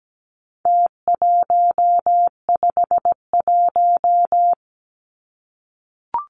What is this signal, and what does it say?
Morse code, heard as a keyed tone, about 17 wpm, 700 Hz. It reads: T151
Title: Numbers - Contest Style - 4-Digit 17wpm